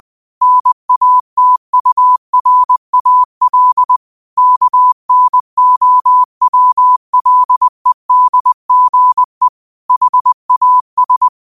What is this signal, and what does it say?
Morse code, heard as a keyed tone, about 20 wpm, 1000 Hz. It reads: NATURAL KNOWLEDGE HAS